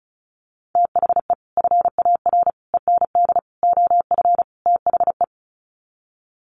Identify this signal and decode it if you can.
Morse code, heard as a keyed tone, about 34 wpm, 700 Hz. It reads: THE FAR END OF THE
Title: Sets of 5 Words 35wpm